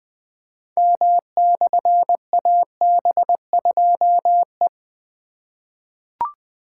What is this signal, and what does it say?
Morse code, heard as a keyed tone, about 20 wpm, 700 Hz. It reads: M/AB2E
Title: Difficult Call Signs 20wpm